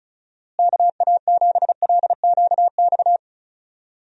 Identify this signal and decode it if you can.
Morse code, heard as a keyed tone, about 36 wpm, 700 Hz. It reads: KA7LQX
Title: USA Call Signs - 6 Characters - Repeated 2x - 35wpm